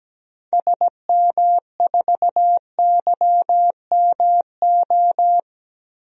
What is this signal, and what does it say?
Morse code, heard as a keyed tone, about 17 wpm, 700 Hz. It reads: SM4YMO
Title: DXCC-100 Call Signs - 6 Characters - 17wpm